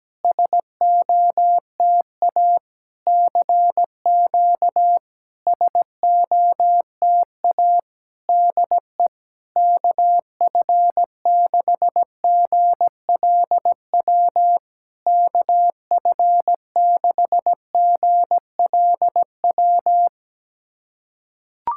Morse code, heard as a keyed tone, about 17 wpm, 700 Hz. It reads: SOTA CQ SOTA DE KF6GLW KF6GLW
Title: SOTA QSO Practice 17wpm